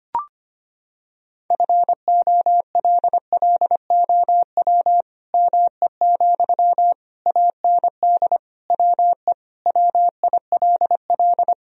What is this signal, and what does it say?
Morse code, heard as a keyed tone, about 25 wpm, 700 Hz. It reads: FOLLOW ME, AND WE WILL
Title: Sentences from Top 1000 Words - Review Part 3 - 25wpm